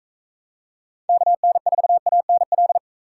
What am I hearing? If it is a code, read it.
Morse code, heard as a keyed tone, about 39 wpm, 700 Hz. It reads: KN4ANL